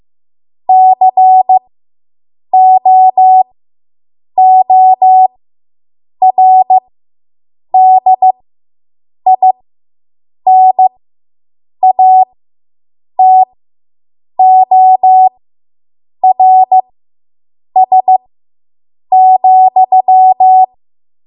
Morse code, heard as a keyed tone, about 15 wpm, spaced as for 7 wpm, 750 Hz. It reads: COORDINATORS,